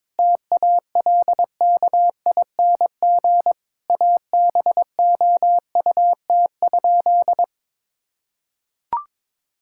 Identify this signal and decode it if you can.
Morse code, heard as a keyed tone, about 22 wpm, 700 Hz. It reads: TALKING ABOUT?
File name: practice / 22wpm / Sentences with 2-Word Phrases - 22wpm